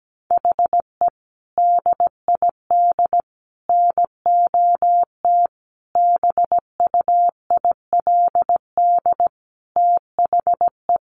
Morse code, heard as a keyed tone, about 17 wpm, 700 Hz. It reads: HE DID NOT BUILD THE
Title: Sentences from Top 1200 Words 17wpm